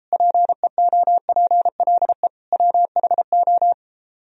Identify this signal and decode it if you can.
Morse code, heard as a keyed tone, about 33 wpm, 700 Hz. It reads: PEOPLE WHO